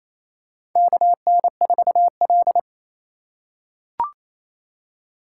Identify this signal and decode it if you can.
Morse code, heard as a keyed tone, about 27 wpm, 700 Hz. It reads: KN4L
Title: USA Call Signs - 4x Character Spacing - 28wpm